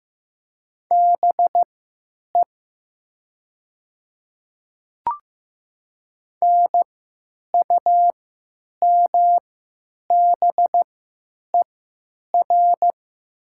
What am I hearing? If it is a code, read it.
Morse code, heard as a keyed tone, about 15 wpm, spaced as for 9 wpm, 700 Hz. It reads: BE NUMBER
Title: 3x Character Spacing - Top 100 Words - 15wpm